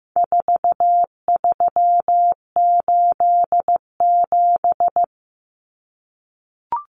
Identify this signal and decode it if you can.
Morse code, heard as a keyed tone, about 15 wpm, 700 Hz. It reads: 4387
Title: Numbers - 4-Digit 15wpm